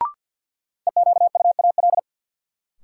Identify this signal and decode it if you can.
Morse code, heard as a keyed tone, about 51 wpm, 700 Hz. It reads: EQUAL
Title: Sentences from Top 1400 Words - New Word Drill Alt - 50wpm